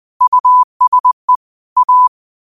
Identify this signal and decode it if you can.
Morse code, heard as a keyed tone, about 20 wpm, 1000 Hz. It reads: USE A
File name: qotdfiles / 20wpm